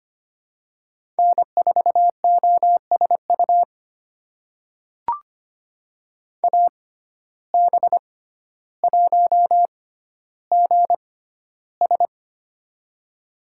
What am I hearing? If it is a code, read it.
Morse code, heard as a keyed tone, about 25 wpm, spaced as for 13 wpm, 700 Hz. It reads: N4OSU AB1GS
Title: USA Call Signs - 5 Characters - 6x Character Spacing - 25wpm